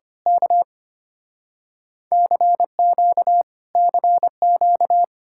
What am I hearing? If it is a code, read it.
Morse code, heard as a keyed tone, about 24 wpm, 700 Hz. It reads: K CQ CQ
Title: Speed-Racing Common QSO Phrases - 25wpm